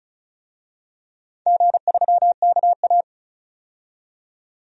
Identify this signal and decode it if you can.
Morse code, heard as a keyed tone, about 34 wpm, 700 Hz. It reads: G3KA
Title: DXCC-100 Call Signs - 4 Characters - 35wpm